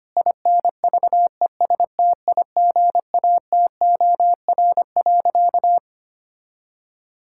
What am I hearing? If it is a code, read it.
Morse code, heard as a keyed tone, about 25 wpm, 700 Hz. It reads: INVESTIGATOR.
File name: practice / 25wpm / All-Age Appropriate Jokes - 25wpm